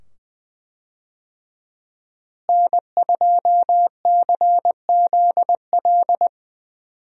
Morse code, heard as a keyed tone, about 20 wpm, 700 Hz. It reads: N2CZL
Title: USA Call Signs - 5 Characters - Repeated 2x - 20wpm